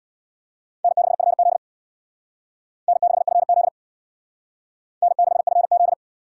Morse code, heard as a keyed tone, about 60 wpm, 690 Hz. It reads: N6FZ N6FZ N6FZ
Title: Speed Racing - USA 1x2 Call Signs 50wpm